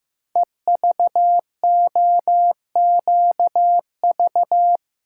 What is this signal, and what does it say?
Morse code, heard as a keyed tone, about 15 wpm, 700 Hz. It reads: EVOQV
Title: Code Groups - 5 Letters - 15wpm